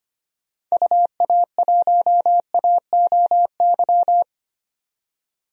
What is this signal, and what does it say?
Morse code, heard as a keyed tone, about 25 wpm, 700 Hz. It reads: UA1AOY
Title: DXCC-100 Call Signs - 6 Characters - 25wpm